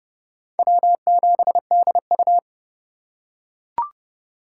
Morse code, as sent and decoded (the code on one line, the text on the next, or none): .-- --... -.. ..-
W7DU